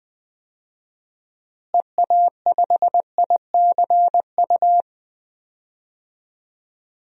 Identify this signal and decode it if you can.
Morse code, heard as a keyed tone, about 20 wpm, 700 Hz. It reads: EA5ICU
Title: DXCC-100 Call Signs - 6 Characters - Repeated 3x - 20wpm